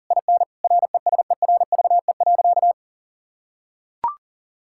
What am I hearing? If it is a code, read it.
Morse code, heard as a keyed tone, about 37 wpm, 700 Hz. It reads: IN RESERVE.
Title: Sentences from Top 1000 Words - Review Part 6 - 40wpm